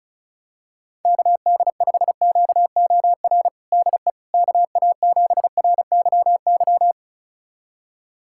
Morse code, heard as a keyed tone, about 36 wpm, 700 Hz. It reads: KD5QOR DE KA7RYY
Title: USA Callsign DE Callsign - 6 Characters - 35wpm